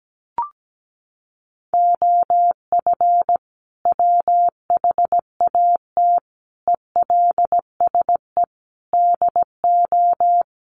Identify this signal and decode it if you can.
Morse code, heard as a keyed tone, about 17 wpm, 700 Hz. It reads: OF WHAT ELSE DO
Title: Sentences from Top 1400 Words 17wpm